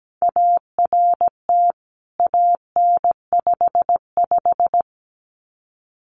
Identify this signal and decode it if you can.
Morse code, heard as a keyed tone, about 17 wpm, 700 Hz. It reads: ART AN55
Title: CWT Exchange 17wpm